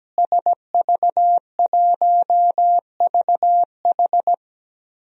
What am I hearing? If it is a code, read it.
Morse code, heard as a keyed tone, about 17 wpm, 700 Hz. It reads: SV1VH
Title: DXCC-100 Call Signs - 5 Characters - Repeated 2x - 17wpm